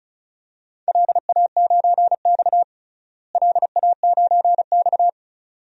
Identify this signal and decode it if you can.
Morse code, heard as a keyed tone, about 36 wpm, 700 Hz. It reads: LA9X LA9X
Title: DXCC-100 Call Signs - 4 Characters - Repeated 2x - 35wpm